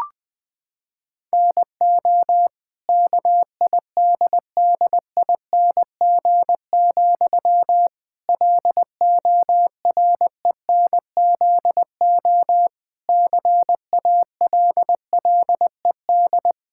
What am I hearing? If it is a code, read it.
Morse code, heard as a keyed tone, about 20 wpm, 700 Hz. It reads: NO KIDDING, LORENZO CALLED
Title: Pangram Challenge v2 - 20wpm